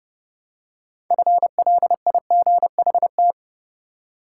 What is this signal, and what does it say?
Morse code, heard as a keyed tone, about 29 wpm, 700 Hz. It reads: FLIGHT
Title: Six Letter Words - 30wpm